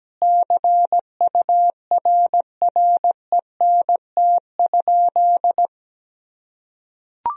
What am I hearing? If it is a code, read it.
Morse code, heard as a keyed tone, about 17 wpm, 700 Hz. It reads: CURRENT?
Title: Sentences from Top 1000 Words - Review Part 4 - 17wpm